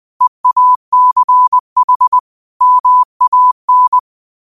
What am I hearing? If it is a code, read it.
Morse code, heard as a keyed tone, about 20 wpm, 1000 Hz. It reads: EACH MAN
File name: qotdfiles / 20wpm